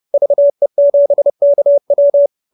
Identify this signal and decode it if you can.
Morse code, heard as a keyed tone, about 30 wpm, 570 Hz. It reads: VE7KW
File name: QSO314 30